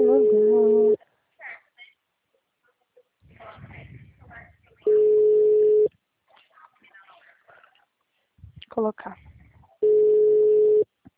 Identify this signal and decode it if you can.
Ringback tone (Europe, 425 Hz), with speech.
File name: out- -2010- - - .257